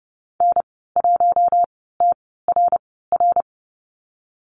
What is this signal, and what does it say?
Morse code, heard as a keyed tone, about 30 wpm, spaced as for 17 wpm, 700 Hz. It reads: N1TRR
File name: Lesson 14g - Number 2 - Call Signs 30wpm